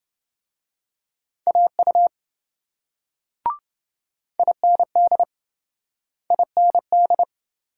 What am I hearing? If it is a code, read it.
Morse code, heard as a keyed tone, about 28 wpm, 700 Hz. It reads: AU IND IND